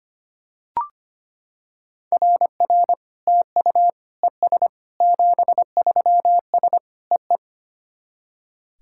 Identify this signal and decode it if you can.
Morse code, heard as a keyed tone, about 25 wpm, 690 Hz. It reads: RR TU ES 73S EE
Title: SOTA QSO Practice 25wpm